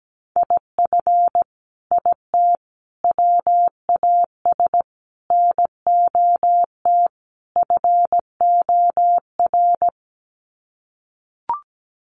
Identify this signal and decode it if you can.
Morse code, heard as a keyed tone, about 17 wpm, 700 Hz. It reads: IF IT WAS NOT FOR
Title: Sets of 5 Words Encore 17wpm